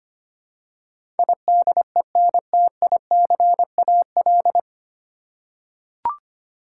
Morse code, heard as a keyed tone, about 25 wpm, 700 Hz. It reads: IDENTICAL
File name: practice / 25wpm / 9 Letter Words - First 3 Letters Given - 25wpm